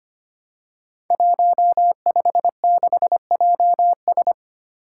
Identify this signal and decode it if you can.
Morse code, heard as a keyed tone, about 25 wpm, 700 Hz. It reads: 156JS
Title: Five Challenging Letter-Number Combinations 25wpm